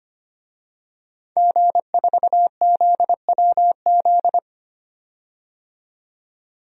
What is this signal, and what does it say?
Morse code, heard as a keyed tone, about 25 wpm, 700 Hz. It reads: G4ZWZ
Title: DXCC-100 Call Signs - 5 Characters - 25wpm